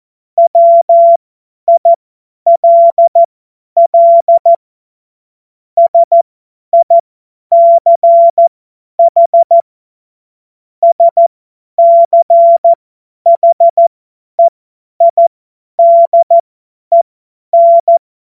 Morse code, heard as a keyed tone, about 14 wpm, spaced as for 10 wpm, 680 Hz. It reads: WILL SICH SCHEIDEN
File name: Morsekurs / Witze / Witz 01. November 2022 50 F70